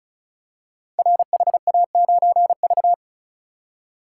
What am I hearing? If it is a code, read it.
Morse code, heard as a keyed tone, about 36 wpm, 700 Hz. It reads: RHA9V
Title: Code Groups - 5 Letter:Numbers - 35wpm